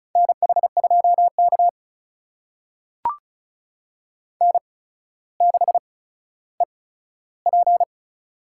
Morse code, heard as a keyed tone, about 36 wpm, spaced as for 15 wpm, 700 Hz. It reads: NH2K N6EP
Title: USA Call Signs - 4 Characters - 8x Character Spacing - 35wpm